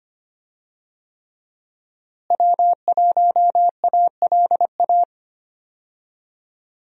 Morse code, heard as a keyed tone, about 25 wpm, 700 Hz. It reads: W1ALA
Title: USA Call Signs - 5 Characters - Repeated 3x - 25wpm